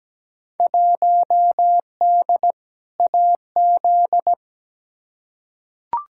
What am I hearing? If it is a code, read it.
Morse code, heard as a keyed tone, about 17 wpm, 700 Hz. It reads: 1D AZ
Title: Field Day Exchanges - 17wpm